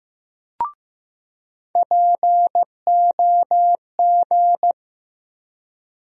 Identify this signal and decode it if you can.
Morse code, heard as a keyed tone, about 15 wpm, 700 Hz. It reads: POG
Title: USA Call Sign Postfixes - 3 Letters 15wpm